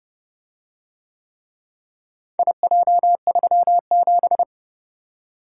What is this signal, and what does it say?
Morse code, heard as a keyed tone, about 28 wpm, 700 Hz. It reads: IJ37